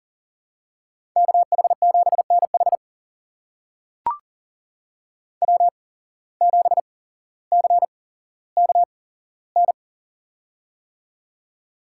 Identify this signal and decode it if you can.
Morse code, heard as a keyed tone, about 38 wpm, spaced as for 17 wpm, 700 Hz. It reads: KH7NH W7CKN
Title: USA Call Signs - 5 Characters - 8x Character Spacing - 40wpm